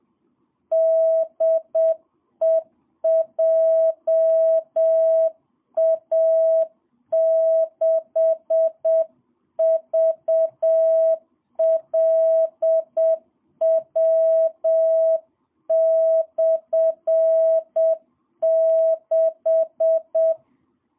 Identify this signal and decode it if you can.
Morse code, heard as a keyed tone, about 7 wpm, 660 Hz. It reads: DE JA6VLW/6
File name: jh8cbh / 2022cw